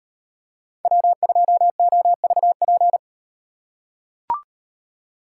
Morse code, heard as a keyed tone, about 38 wpm, 700 Hz. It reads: W2OVP